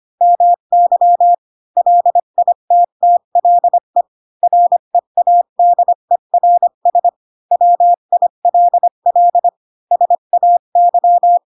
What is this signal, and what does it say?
Morse code, heard as a keyed tone, about 25 wpm, 700 Hz. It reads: MY LITTLE READERS WILL SAY
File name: Pinocchio / Pinocchio ch1 20 25 ik1hge